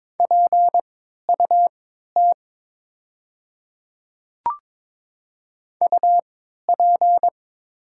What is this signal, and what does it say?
Morse code, heard as a keyed tone, about 22 wpm, spaced as for 12 wpm, 700 Hz. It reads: PUT UP
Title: 3x Character Spacing - Top 200 Words - 22wpm